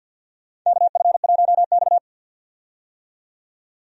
Morse code, heard as a keyed tone, about 48 wpm, 700 Hz. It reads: KF1X
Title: USA Call Signs - 4 Characters - 50wpm